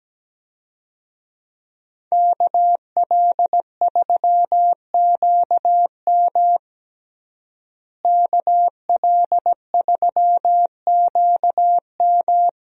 Morse code, heard as a keyed tone, about 17 wpm, 700 Hz. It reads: KL3QM KL3QM
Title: USA Call Signs - 5 Characters - Repeated 2x - 17wpm